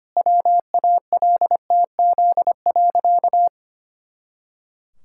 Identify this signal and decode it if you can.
Morse code, heard as a keyed tone, about 25 wpm, 700 Hz. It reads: WALTZ.
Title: Pangram Challenge v2 - 25wpm